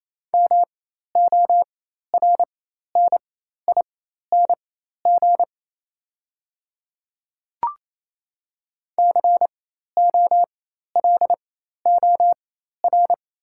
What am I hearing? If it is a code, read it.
Morse code, heard as a keyed tone, about 28 wpm, spaced as for 13 wpm, 690 Hz. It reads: MORNING COLOR
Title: 4x Character Spacing - Top 500 Words - 28wpm